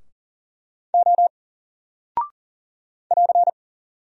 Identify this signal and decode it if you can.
Morse code, heard as a keyed tone, about 40 wpm, 700 Hz. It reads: O <AR>
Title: Single Character (Letter:Number:Punctuation:Prosign) 40wpm